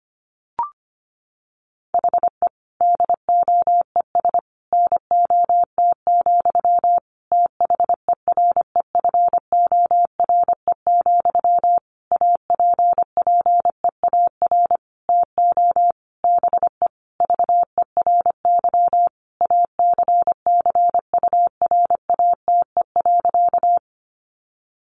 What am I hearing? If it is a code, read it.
Morse code, heard as a keyed tone, about 25 wpm, 700 Hz. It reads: HE DOES NOT, THEREFORE, APPEAR TO BE VERY ACCURATE.
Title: Sentences from Top 1200 Words 25wpm